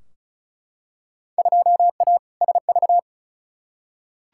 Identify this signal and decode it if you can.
Morse code, heard as a keyed tone, about 36 wpm, 700 Hz. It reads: 2A SV
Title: Field Day Exchanges - 35wpm